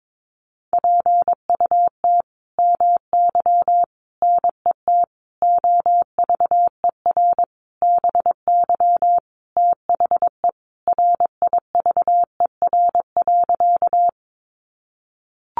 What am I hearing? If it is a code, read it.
Morse code, heard as a keyed tone, about 22 wpm, 700 Hz. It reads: PUT MY NET OVER BY THE RIVER.
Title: Sentences from Top 1400 Words 22wpm